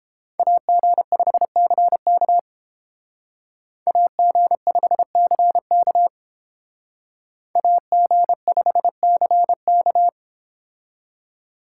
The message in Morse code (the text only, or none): AG5CK AG5CK AG5CK